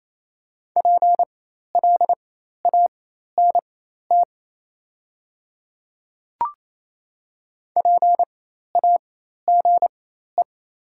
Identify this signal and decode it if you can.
Morse code, heard as a keyed tone, about 28 wpm, spaced as for 13 wpm, 700 Hz. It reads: PLANT PAGE
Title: 4x Character Spacing - Top 300 Words - 28wpm